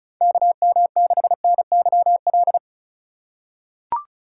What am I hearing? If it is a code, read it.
Morse code, heard as a keyed tone, about 36 wpm, 700 Hz. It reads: KM6NYL
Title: USA Call Signs - 6 Characters - 2x Character Spacing - 35wpm